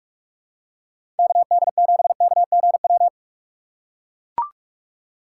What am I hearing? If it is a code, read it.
Morse code, heard as a keyed tone, about 44 wpm, 700 Hz. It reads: KD7KGW
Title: USA Call Signs - 6 Characters - 6x Character Spacing - 45wpm